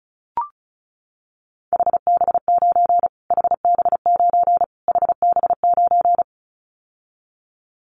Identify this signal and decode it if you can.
Morse code, heard as a keyed tone, about 34 wpm, 700 Hz. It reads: HB9 HB9 HB9
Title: Callsign Prefixes for Top 25 DXCC - Repeated 3x - 35wpm